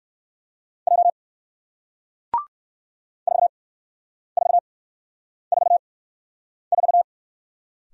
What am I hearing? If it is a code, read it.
Morse code, heard as a keyed tone, about 58 wpm, spaced as for 9 wpm, 700 Hz. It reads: J 4444